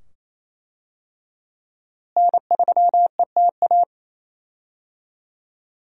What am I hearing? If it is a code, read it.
Morse code, heard as a keyed tone, about 28 wpm, 700 Hz. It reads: N3ETA
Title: USA Call Signs - 5 Characters - Repeated 2x - 28wpm